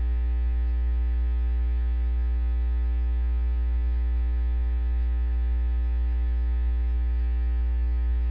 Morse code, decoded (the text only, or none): none